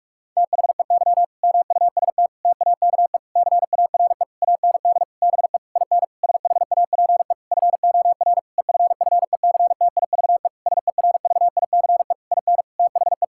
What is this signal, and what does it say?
Morse code, heard as a keyed tone, about 46 wpm, 700 Hz. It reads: THEY MUST TAKE CARE AND BE IN SHAPE FOR EFFECTIVE SERVICE IN THE
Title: Sentences from Top 1000 Words - Review Part 6 - 45wpm